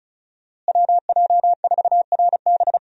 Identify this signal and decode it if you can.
Morse code, heard as a keyed tone, about 36 wpm, 700 Hz. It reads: WJ4RB
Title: USA Call Signs - 5 Characters - Repeated 3x - 35wpm